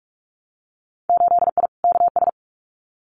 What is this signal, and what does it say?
Morse code, heard as a keyed tone, about 46 wpm, 700 Hz. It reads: 8I KS
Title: Winter Field Day Exchanges Encore 45wpm